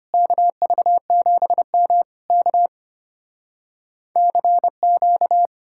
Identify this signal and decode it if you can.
Morse code, heard as a keyed tone, about 33 wpm, 700 Hz. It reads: KV7M K CQ